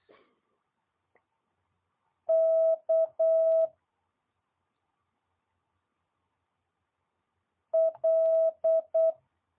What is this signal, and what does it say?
Morse code, heard as a keyed tone, about 8 wpm, 650 Hz. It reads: K L
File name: jh8cbh / 2022cw